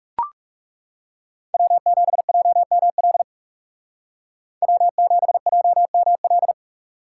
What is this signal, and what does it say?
Morse code, heard as a keyed tone, about 39 wpm, 700 Hz. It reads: W7JML W7JML